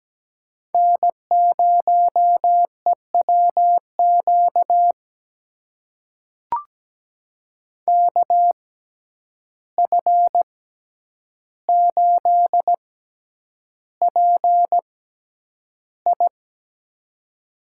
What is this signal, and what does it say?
Morse code, heard as a keyed tone, about 17 wpm, spaced as for 9 wpm, 700 Hz. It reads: N0EWQ KF8PI
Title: USA Call Signs - 5 Characters - 6x Character Spacing - 17wpm